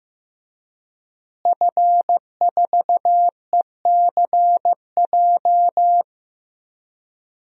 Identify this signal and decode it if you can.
Morse code, heard as a keyed tone, about 15 wpm, 700 Hz. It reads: F4ECJ
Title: DXCC-100 Call Signs - 5 Characters - Repeated 2x - 15wpm